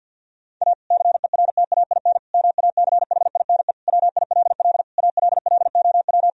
Morse code, heard as a keyed tone, about 50 wpm, 700 Hz. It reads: A CERTAIN MACHINE WILL ALLOW
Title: Sentences from Top 1000 Words - Review Part 2 - 50wpm